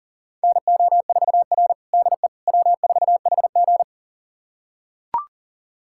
Morse code, heard as a keyed tone, about 37 wpm, 700 Hz. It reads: NO4R DE W4HG
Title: USA Callsign DE Callsign - 4 Characters - 40wpm